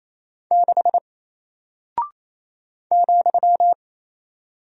Morse code, heard as a keyed tone, about 27 wpm, 700 Hz. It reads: 6 ,
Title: Single Character (Letter:Number:Punctuation:Prosign) 28wpm